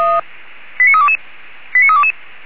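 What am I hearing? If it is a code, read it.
Five-tone selective call (ZVEI), heard as quick runs of five tones, tones of about 70 ms. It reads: 89322 89322
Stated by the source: Received on SDR; ZVEI five-tone call 89322, channel 464.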